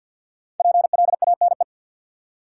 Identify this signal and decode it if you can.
Morse code, heard as a keyed tone, about 53 wpm, 700 Hz. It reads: PLANE
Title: Five Letter Words - 50wpm